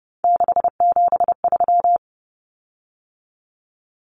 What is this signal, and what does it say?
Morse code, heard as a keyed tone, about 30 wpm, 700 Hz. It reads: 673
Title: Common Cut Numbers - 3-Digit 30wpm